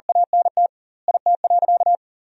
Morse code, heard as a keyed tone, about 40 wpm, 700 Hz. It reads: ANT IT.
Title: 2x Word-Spacing - Sentences from Top 300 Words - 40wpm